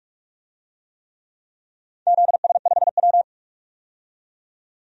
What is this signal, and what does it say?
Morse code, heard as a keyed tone, about 41 wpm, 700 Hz. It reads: ZS5W
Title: DXCC-100 Call Signs - 4 Characters - 45wpm